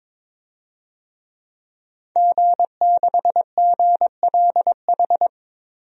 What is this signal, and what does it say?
Morse code, heard as a keyed tone, about 22 wpm, 700 Hz. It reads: G6GLH
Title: DXCC-100 Call Signs - 5 Characters - 22wpm